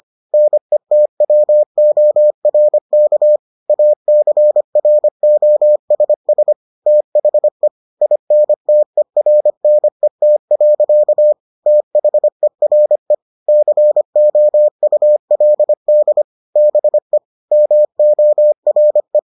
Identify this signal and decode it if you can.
Morse code, heard as a keyed tone, about 25 wpm, 600 Hz. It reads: NETWORK ACROSS THE INTERNET. THERE COULD BE MORE